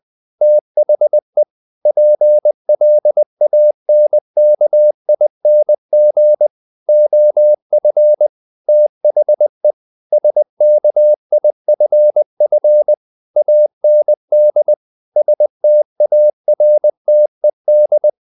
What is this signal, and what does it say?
Morse code, heard as a keyed tone, about 20 wpm, 600 Hz. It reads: THE PLANKING OF THE SKIFF AND STARTED
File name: the old man and the sea part 2 20wpm